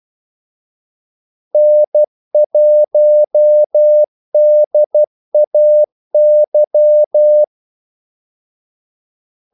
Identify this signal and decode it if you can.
Morse code, heard as a keyed tone, about 12 wpm, 600 Hz. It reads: N1DAY
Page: 2 ekstra ordmellemrum mellem kaldesignalerne